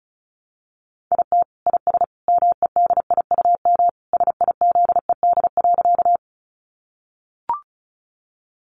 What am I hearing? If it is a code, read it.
Morse code, heard as a keyed tone, about 34 wpm, 700 Hz. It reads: IT IS MEDIUM SIZED.
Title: Easy Phrases 35wpm